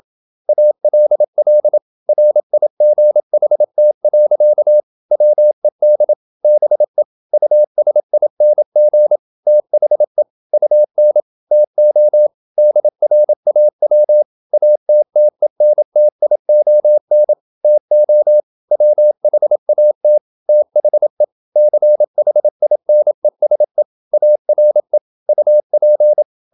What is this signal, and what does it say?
Morse code, heard as a keyed tone, about 27 wpm, 600 Hz. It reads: ALL RIGHT. WED BE USING THE UN TO DRAW ATTENTION TO WHAT THE CHINESE ARE UP
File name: Never 0027wpm